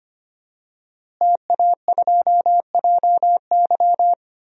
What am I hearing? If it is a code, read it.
Morse code, heard as a keyed tone, about 25 wpm, 700 Hz. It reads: TA2JY
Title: DXCC-100 Call Signs - 5 Characters - Repeated 2x - 25wpm